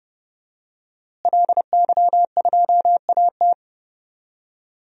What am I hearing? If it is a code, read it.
Morse code, heard as a keyed tone, about 30 wpm, 700 Hz. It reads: LY2AT
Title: DXCC-100 Call Signs - 5 Characters - Repeated 2x - 30wpm